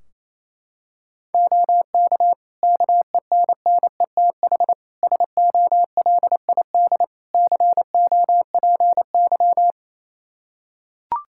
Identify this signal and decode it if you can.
Morse code, heard as a keyed tone, about 28 wpm, 700 Hz. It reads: OK KENNETH SOLID COPY
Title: Practice QSO - Exchange 3 - More Info - 28wpm